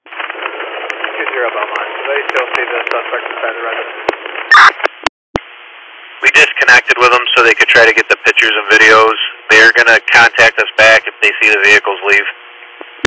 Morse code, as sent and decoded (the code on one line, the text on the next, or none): none